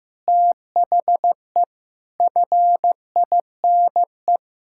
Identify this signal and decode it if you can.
Morse code, heard as a keyed tone, about 15 wpm, 700 Hz. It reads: THE FINE
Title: Pangram Challenge v2 - 15wpm